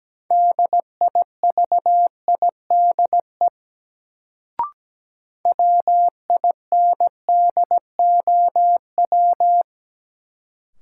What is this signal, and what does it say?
Morse code, heard as a keyed tone, about 17 wpm, 700 Hz. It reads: DIVIDE WINDOW
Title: Sentences from Top 1600 Words - New Word Drill Alt - 17wpm